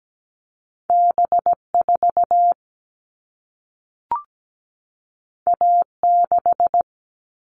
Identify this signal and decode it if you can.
Morse code, heard as a keyed tone, about 17 wpm, 700 Hz. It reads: B4 A6
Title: All Cut Numbers - 2-Digit 17wpm